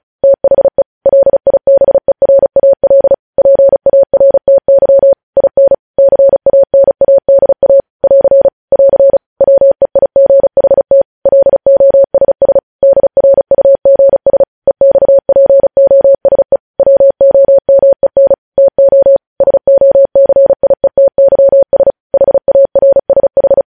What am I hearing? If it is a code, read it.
Morse code, heard as a keyed tone, about 36 wpm, 550 Hz. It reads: THE LIBERAL PARTY IN CANADA <AR> <AR> WEIGHT LOSS DRUGS EXPOSE WOMEN TO SOCIETYS HARSH
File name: podcast / News headlines-1x-35x35